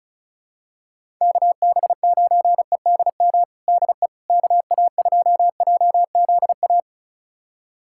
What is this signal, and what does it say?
Morse code, heard as a keyed tone, about 36 wpm, 700 Hz. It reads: KB9EDM DE KA2JZA
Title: USA Callsign DE Callsign 35wpm